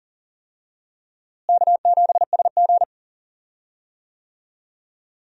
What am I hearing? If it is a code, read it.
Morse code, heard as a keyed tone, about 40 wpm, 700 Hz. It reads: K7SG
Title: USA Call Signs - 4 Characters - Repeated 3x - 40wpm